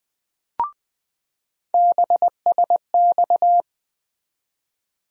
Morse code, heard as a keyed tone, about 20 wpm, 700 Hz. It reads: BSX
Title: USA Call Sign Postfixes - 3 Letters 20wpm